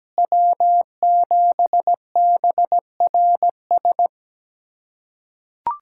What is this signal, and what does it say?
Morse code, heard as a keyed tone, about 17 wpm, 700 Hz. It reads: W7BRS
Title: USA Callsign DE Callsign - 5 Characters - 17wpm